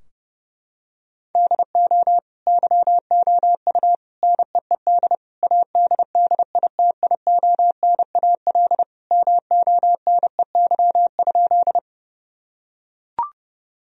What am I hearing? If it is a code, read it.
Morse code, heard as a keyed tone, about 30 wpm, 700 Hz. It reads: DO YOU NEED ADDITIONAL MONEY?
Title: Sentences from Top 1000 Words - Review Part 4 - 30wpm